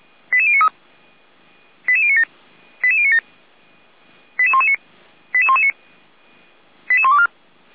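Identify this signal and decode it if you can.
Five-tone selective call (ZVEI), heard as quick runs of five tones, tones of about 80 ms. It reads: 88093 88097 88097 88119 88119 88124
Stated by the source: ZVEI five-tone selective call 88057, received via SDR.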